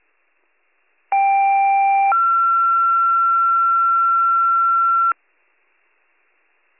Two-tone page, heard as a one-second tone, then a three-second tone, about 790 then 1360 Hz. It is heard from 1.1 to 5.1 s.